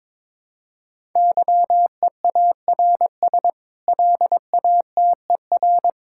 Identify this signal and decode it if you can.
Morse code, heard as a keyed tone, about 22 wpm, 700 Hz. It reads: YEARS LATER
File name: practice / 22wpm / Sets of 8 Words - 22wpm